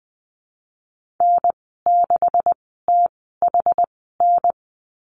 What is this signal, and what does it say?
Morse code, heard as a keyed tone, about 20 wpm, 700 Hz. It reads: N 6 T H N
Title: 238 Lesson 38g - Letter q - Call Signs 20wpm